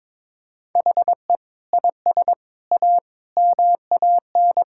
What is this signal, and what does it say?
Morse code, heard as a keyed tone, about 22 wpm, 700 Hz. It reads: HE IS A MAN
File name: practice / 22wpm / Sentences with 3-Word Phrases - 22wpm